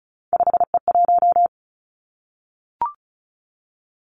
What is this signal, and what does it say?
Morse code, heard as a keyed tone, about 36 wpm, 700 Hz. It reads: 5E1
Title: Three Challenging Letter-Number Combinations 35wpm